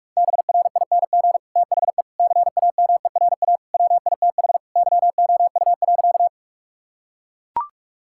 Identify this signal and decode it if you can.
Morse code, heard as a keyed tone, about 41 wpm, 700 Hz. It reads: BRING THE CAMERA WITH YOU.
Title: Sentences from Top 1000 Words - Review Part 5 - 45wpm